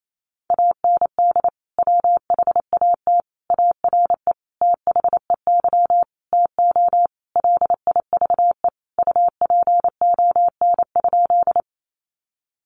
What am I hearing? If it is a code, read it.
Morse code, heard as a keyed tone, about 28 wpm, 700 Hz. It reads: AND WHAT ARE THEY TO LIVE UPON?
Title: Sentences from Top 1100 Words 28wpm